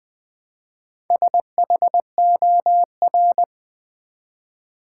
Morse code, heard as a keyed tone, about 20 wpm, 700 Hz. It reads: SHOR
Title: Build A Word - 1 Letter At A Time - 3 to 5-Letter Words - 20wpm